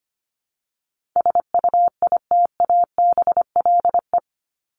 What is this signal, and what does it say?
Morse code, heard as a keyed tone, about 25 wpm, 700 Hz. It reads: SUITABLE
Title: Top 1251-1500 Words 25wpm